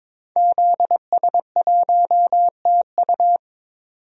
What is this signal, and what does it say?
Morse code, heard as a keyed tone, about 22 wpm, 700 Hz. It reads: ZS1TU
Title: DXCC-100 Call Signs - 5 Characters - 22wpm